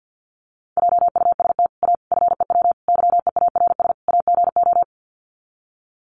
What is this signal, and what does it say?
Morse code, heard as a keyed tone, about 50 wpm, 700 Hz. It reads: JUST A FEW YEARS AGO
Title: Sets of 5 Words Encore 50wpm